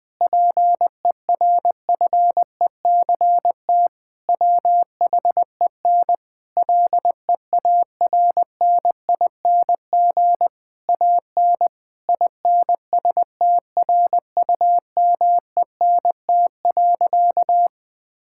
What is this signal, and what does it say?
Morse code, heard as a keyed tone, about 21 wpm, 700 Hz. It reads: PERFECT WHEN LEARNING AN INSTRUMENT.
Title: Difficult Sentences with Hints - 20wpm